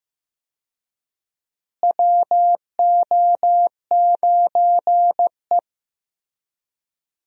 Morse code, heard as a keyed tone, about 15 wpm, 700 Hz. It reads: WO9E
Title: USA Call Signs - 4 Characters - Repeated 2x - 15wpm